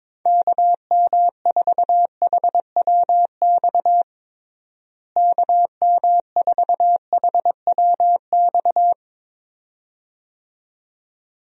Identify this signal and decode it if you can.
Morse code, heard as a keyed tone, about 22 wpm, 700 Hz. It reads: KM4HWX KM4HWX
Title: USA Call Signs - 6 Characters - Repeated 2x - 22wpm